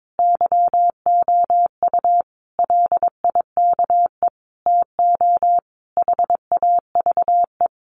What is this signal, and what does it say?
Morse code, heard as a keyed tone, about 22 wpm, 700 Hz. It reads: YOU LIKE TO HAVE
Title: Sentences from Top 100 Words 22wpm